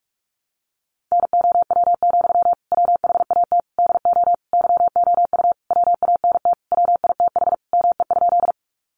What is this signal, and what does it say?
Morse code, heard as a keyed tone, about 44 wpm, 700 Hz. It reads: NOW, WHAT DO YOU WANT WITH ME?
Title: Sentences from Top 200 Words 45wpm